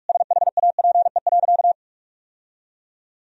Morse code, heard as a keyed tone, about 42 wpm, 700 Hz. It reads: SHAPE.
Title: Sentences from Top 1000 Words - Review Part 3 - 45wpm